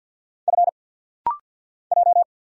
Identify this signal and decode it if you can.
Morse code, heard as a keyed tone, about 50 wpm, 700 Hz. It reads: F J
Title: Single Letter-Number Alt - Rapid-Fire 50wpm